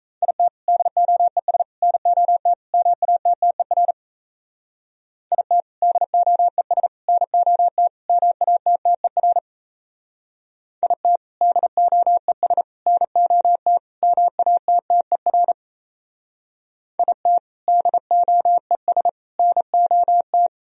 Morse code, heard as a keyed tone, about 35 wpm, 700 Hz. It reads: IT DOES NOT MATTER IT DOES NOT MATTER IT DOES NOT MATTER IT DOES NOT